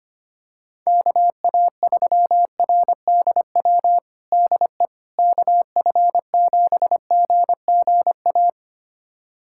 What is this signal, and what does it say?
Morse code, heard as a keyed tone, about 25 wpm, 700 Hz. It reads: KA3RDW DE KF7GGA
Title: USA Callsign DE Callsign 25wpm